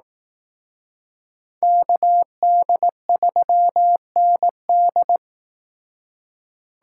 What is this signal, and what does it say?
Morse code, heard as a keyed tone, about 18 wpm, 700 Hz. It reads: KD3ND
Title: Speed Racing - USA 2x2 Call Signs 15wpm